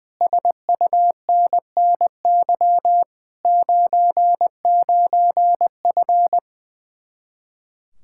Morse code, heard as a keyed tone, about 21 wpm, 700 Hz. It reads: SUNNY 99F
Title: Speed-Racing Common QSO Phrases - 20wpm